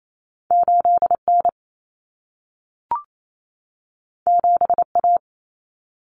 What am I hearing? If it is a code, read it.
Morse code, heard as a keyed tone, about 28 wpm, 700 Hz. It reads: 8N 7A
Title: Common Cut Numbers - 2-Digit 28wpm